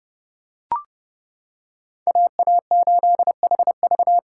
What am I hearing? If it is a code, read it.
Morse code, heard as a keyed tone, about 29 wpm, 700 Hz. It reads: AA8HV